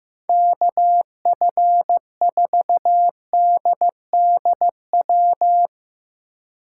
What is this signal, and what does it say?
Morse code, heard as a keyed tone, about 15 wpm, 700 Hz. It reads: KF4DDW
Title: USA Call Signs - 6 Characters - 2x Character Spacing - 15wpm